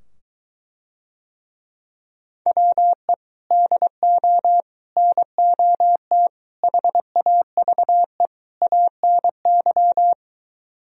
Morse code, heard as a keyed tone, about 24 wpm, 700 Hz. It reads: WE DO NOT HAVE ANY